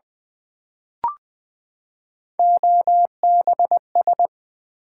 Morse code, heard as a keyed tone, about 20 wpm, 700 Hz. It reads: OBS
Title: Build A Word - 3 Letters At A Time - 8-Letter Words - 20wpm